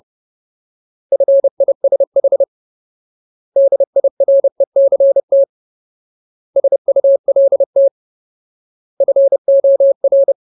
Morse code, heard as a keyed tone, about 30 wpm, 550 Hz. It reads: FISH DIRECT SULT FOR
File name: Ultimate Word List / 30WPM Ultimate